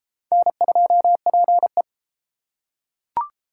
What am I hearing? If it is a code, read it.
Morse code, heard as a keyed tone, about 31 wpm, 700 Hz. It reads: N2PE